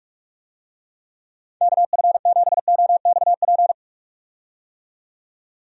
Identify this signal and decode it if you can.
Morse code, heard as a keyed tone, about 46 wpm, 700 Hz. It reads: KF7OXP
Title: USA Call Signs - 6 Characters - 45wpm